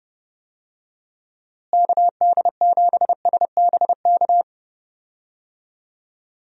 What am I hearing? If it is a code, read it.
Morse code, heard as a keyed tone, about 31 wpm, 700 Hz. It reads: KD7SBK
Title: USA Call Signs - 6 Characters - Repeated 3x - 30wpm